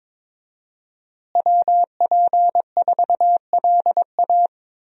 Morse code, heard as a keyed tone, about 22 wpm, 700 Hz. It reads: WP4LA
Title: USA Call Signs - 5 Characters - Repeated 3x - 22wpm